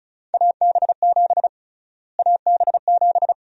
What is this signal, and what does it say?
Morse code, heard as a keyed tone, about 36 wpm, 700 Hz. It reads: AB7 AB7
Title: USA Call Sign Prefixes - Repeated 2x - 35wpm